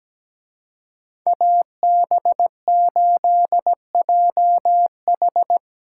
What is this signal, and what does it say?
Morse code, heard as a keyed tone, about 19 wpm, 700 Hz. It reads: AB8JH
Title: Speed Racing - USA 2x2 Call Signs 17wpm